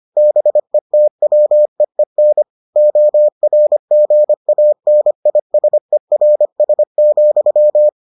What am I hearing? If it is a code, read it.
Morse code, heard as a keyed tone, about 25 wpm, 600 Hz. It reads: BETWEEN ORGANISERS,